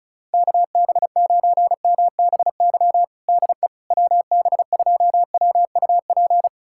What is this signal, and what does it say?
Morse code, heard as a keyed tone, about 34 wpm, 700 Hz. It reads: KB9MBY DE WB2WUP
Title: USA Callsign DE Callsign - 6 Characters - 35wpm